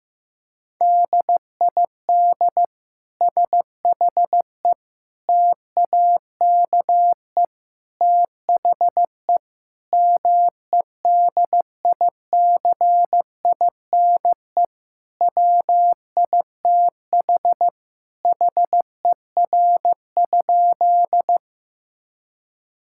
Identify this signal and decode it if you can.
Morse code, heard as a keyed tone, about 16 wpm, 700 Hz. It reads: DID SHE TAKE THE MEDICINE WITH HER?
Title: Sentences from Top 1000 Words - Review Part 6 - 15wpm